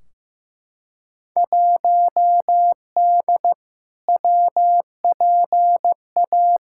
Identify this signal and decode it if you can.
Morse code, heard as a keyed tone, about 16 wpm, 700 Hz. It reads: 1D WPA
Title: Field Day Exchanges - 15wpm